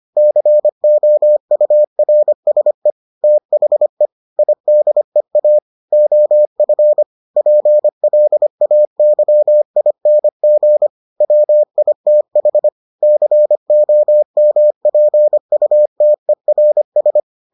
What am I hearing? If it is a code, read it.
Morse code, heard as a keyed tone, about 25 wpm, 600 Hz. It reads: COURSE THE IDEA OF PLAYING WITH COMPUTERS